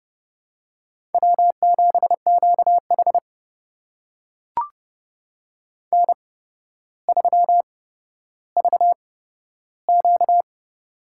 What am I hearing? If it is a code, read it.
Morse code, heard as a keyed tone, about 31 wpm, spaced as for 13 wpm, 700 Hz. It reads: W7QH N3VQ
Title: USA Call Signs - 4 Characters - 8x Character Spacing - 30wpm